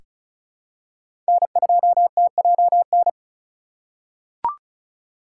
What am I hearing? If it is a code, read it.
Morse code, heard as a keyed tone, about 36 wpm, 700 Hz. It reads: N2TJN
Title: USA Call Signs - 3x Character Spacing - 35wpm